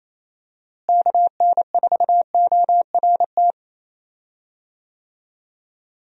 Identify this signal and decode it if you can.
Morse code, heard as a keyed tone, about 28 wpm, 700 Hz. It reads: KN4ORT
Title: USA Call Signs - 6 Characters - Repeated 2x - 28wpm